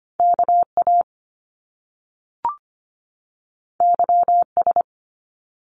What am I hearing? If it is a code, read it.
Morse code, heard as a keyed tone, about 25 wpm, 700 Hz. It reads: KA YS
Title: Top Two Letter Combinations in Words 25wpm